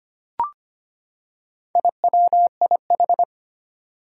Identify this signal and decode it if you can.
Morse code, heard as a keyed tone, about 25 wpm, 700 Hz. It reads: IWIH
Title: Four Challenging Letter-Number Combinations 25wpm